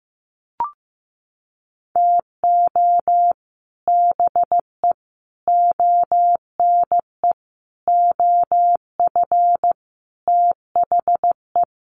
Morse code, heard as a keyed tone, about 15 wpm, 700 Hz. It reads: TO BE ONE OF THE
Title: Sets of 5 Words 15wpm